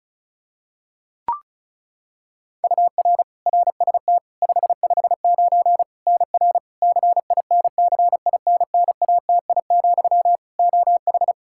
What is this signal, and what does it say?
Morse code, heard as a keyed tone, about 36 wpm, 700 Hz. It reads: UR RST 559 NR CINCINNATI, OH
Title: K1USN SST QSO Practice 35wpm